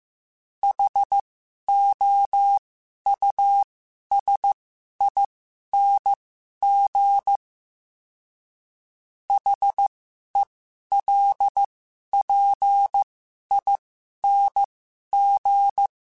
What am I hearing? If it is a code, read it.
Morse code, heard as a keyed tone, about 15 wpm, spaced as for 11 wpm, 800 Hz. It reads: HOUSING HELPING
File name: Special INGs-mp3s / Special INGs-mp3s / Sngl Ing @10